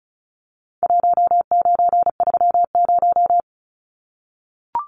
Morse code, heard as a keyed tone, about 34 wpm, 700 Hz. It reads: 1930
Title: Numbers - Contest Style - 4-Digit 35wpm